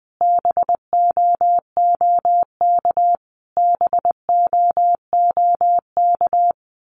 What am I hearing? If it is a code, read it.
Morse code, heard as a keyed tone, about 20 wpm, 700 Hz. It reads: BOOK BOOK
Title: Top 500 Words Repeated 20wpm